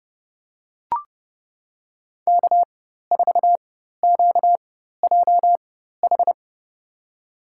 Morse code, heard as a keyed tone, about 30 wpm, spaced as for 14 wpm, 700 Hz. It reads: K4QJH
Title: USA Call Signs - 5 Characters - 4x Character Spacing - 30wpm